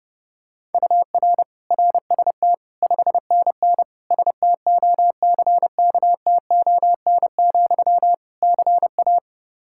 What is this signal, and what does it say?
Morse code, heard as a keyed tone, about 30 wpm, 700 Hz. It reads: UR RST 5NN STOCKTON, CA
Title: K1USN SST QSO Practice 30wpm